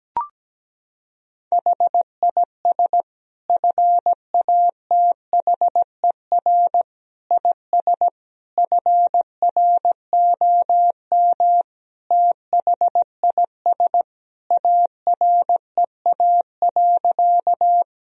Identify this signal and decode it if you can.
Morse code, heard as a keyed tone, about 17 wpm, 700 Hz. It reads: HIS FATHER IS FROM THIS AREA.
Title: Sentences from Top 1000 Words - Review Part 2 - 17wpm